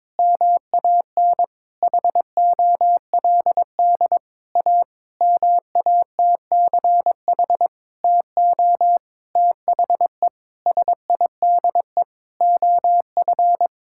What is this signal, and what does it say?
Morse code, heard as a keyed tone, about 22 wpm, 700 Hz. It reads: MAN HOLD A MATCH TO THE SIDE OF
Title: Sentences from Top 1600 Words 22wpm